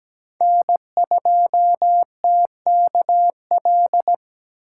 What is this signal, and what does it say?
Morse code, heard as a keyed tone, about 17 wpm, 700 Hz. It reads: N2TKL
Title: USA Call Signs - 5 Characters - 4x Character Spacing - 17wpm